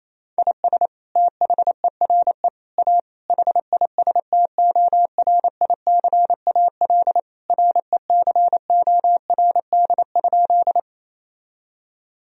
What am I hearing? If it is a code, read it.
Morse code, heard as a keyed tone, about 27 wpm, 700 Hz. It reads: IS THERE A HISTORICAL RECORD?
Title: Sentences from Top 1000 Words - Review Part 4 - 28wpm